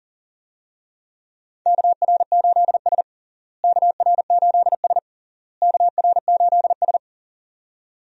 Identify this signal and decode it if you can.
Morse code, heard as a keyed tone, about 37 wpm, 700 Hz. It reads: KR8S KR8S KR8S
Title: USA Call Signs - 4 Characters - Repeated 3x - 40wpm